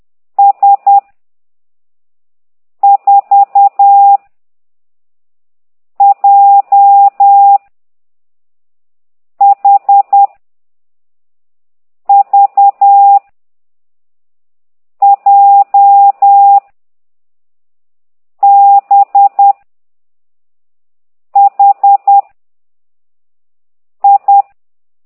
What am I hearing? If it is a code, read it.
Morse code, heard as a keyed tone, about 10 wpm, 800 Hz. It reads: S 4 J H V J B H I
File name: ct2idw / exerc13